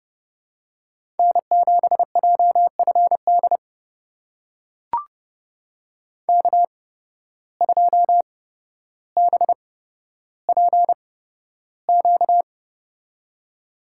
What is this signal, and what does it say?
Morse code, heard as a keyed tone, about 32 wpm, spaced as for 13 wpm, 700 Hz. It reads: N7JFD K2BPQ
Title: USA Call Signs - 5 Characters - 8x Character Spacing - 30wpm